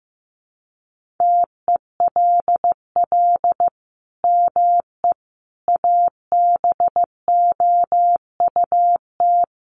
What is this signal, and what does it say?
Morse code, heard as a keyed tone, about 15 wpm, 700 Hz. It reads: TELL ME ABOUT
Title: Sentences from Top 1400 Words 15wpm